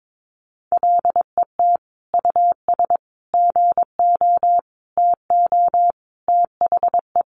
Morse code, heard as a keyed tone, about 22 wpm, 700 Hz. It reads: LET US GO TO THE
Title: Sentences from Top 1400 Words 22wpm